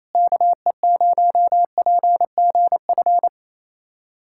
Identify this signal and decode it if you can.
Morse code, heard as a keyed tone, about 28 wpm, 700 Hz. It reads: KE0PGF
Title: USA Callsign DE Callsign - 6 Characters - 28wpm